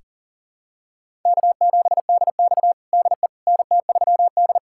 Morse code, heard as a keyed tone, about 40 wpm, 700 Hz. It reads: K7DX DE NT3D
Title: USA Callsign DE Callsign - 4 Characters - 40wpm